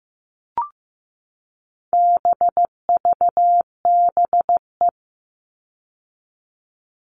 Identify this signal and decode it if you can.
Morse code, heard as a keyed tone, about 15 wpm, 700 Hz. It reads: BVBE
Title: All Cut Numbers - 4-Digit 15wpm